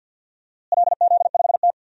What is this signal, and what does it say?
Morse code, heard as a keyed tone, about 48 wpm, 700 Hz. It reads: LZ5T
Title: DXCC-100 Call Signs - 4 Characters - Repeated 2x - 50wpm